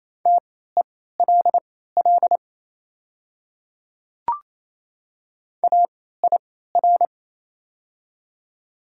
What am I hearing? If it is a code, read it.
Morse code, heard as a keyed tone, about 28 wpm, spaced as for 16 wpm, 700 Hz. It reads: TELL AIR
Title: 3x Character Spacing - Top 200 Words - 28wpm